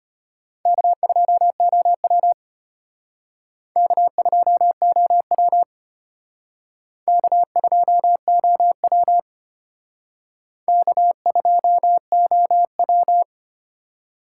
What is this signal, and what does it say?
Morse code, heard as a keyed tone, about 31 wpm, 700 Hz. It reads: K2OW K2OW K2OW K2OW
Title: Speed Racing - USA 1x2 Call Signs 25wpm